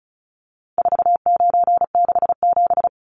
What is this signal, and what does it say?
Morse code, heard as a keyed tone, about 34 wpm, 700 Hz. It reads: 4967
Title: Numbers - 4-Digit 35wpm